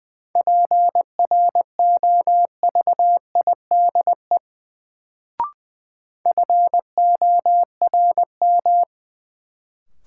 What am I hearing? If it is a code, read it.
Morse code, heard as a keyed tone, about 20 wpm, 700 Hz. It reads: PROVIDE FORM
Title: Sentences from Top 1100 Words - New Word Drill Alt - 20wpm